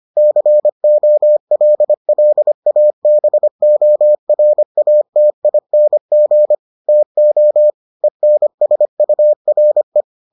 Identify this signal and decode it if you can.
Morse code, heard as a keyed tone, about 25 wpm, 600 Hz. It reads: COLLABORATING TO ENSURE